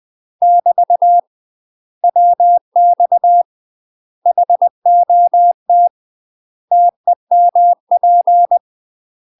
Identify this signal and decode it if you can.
Morse code, heard as a keyed tone, about 20 wpm, 700 Hz. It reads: <BT> WX HOT TEMP
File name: UK MASTER QSOs TRAINING / 20WPM / UK MASTER QSOs 0060